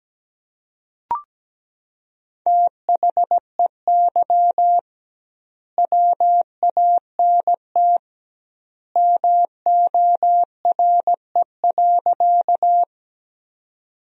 Morse code, heard as a keyed tone, about 17 wpm, 700 Hz. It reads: THEY WANT MORE.
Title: 2x Word-Spacing - Sentences from Top 200 Words - 17wpm